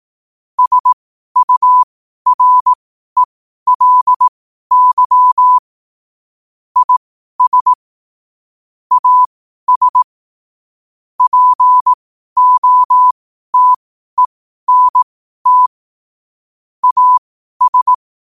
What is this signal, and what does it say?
Morse code, heard as a keyed tone, about 18 wpm, spaced as for 12 wpm, 1000 Hz. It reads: SURELY IS AS POTENT AS
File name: qotdfiles / 12wpm